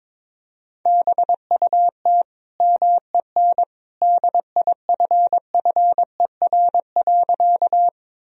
Morse code, heard as a keyed tone, about 22 wpm, 700 Hz. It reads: BUT MEN DIFFER.
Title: Sentences from Top 1000 Words - Review Part 1 - 22wpm